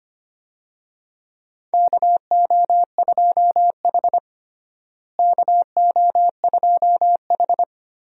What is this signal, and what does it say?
Morse code, heard as a keyed tone, about 25 wpm, 700 Hz. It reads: KO2H KO2H
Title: USA Call Signs - 4 Characters - Repeated 3x - 25wpm